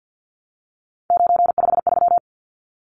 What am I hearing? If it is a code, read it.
Morse code, heard as a keyed tone, about 48 wpm, 700 Hz. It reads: 953
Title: Numbers - 3-Digit 50wpm